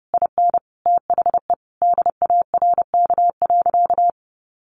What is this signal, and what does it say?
Morse code, heard as a keyed tone, about 30 wpm, 700 Hz. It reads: IN THE DARK.
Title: Sentences from Top 1100 Words 30wpm